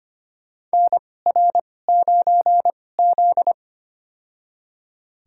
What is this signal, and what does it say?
Morse code, heard as a keyed tone, about 25 wpm, spaced as for 18 wpm, 700 Hz. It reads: NR9Z
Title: USA Call Signs - 4 Characters - 2x Character Spacing - 25wpm